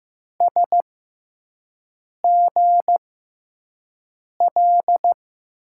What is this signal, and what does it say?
Morse code, heard as a keyed tone, about 16 wpm, spaced as for 5 wpm, 700 Hz. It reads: SGL
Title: USA Call Signs - 6x Character Spacing - 15wpm